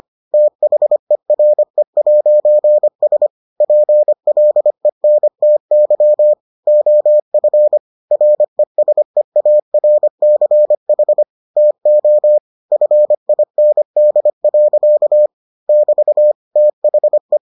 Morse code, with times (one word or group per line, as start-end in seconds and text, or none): 0.3-3.3 THERE'S
3.6-6.3 PLENTY
6.7-7.8 OF
8.1-11.2 RESEARCH
11.6-12.4 TO
12.7-15.3 FIND.
15.7-16.3 <BT>
16.6-17.4 THE